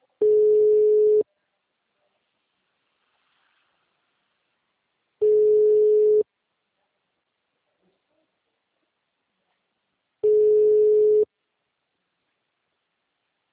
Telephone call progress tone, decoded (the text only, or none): ringback tone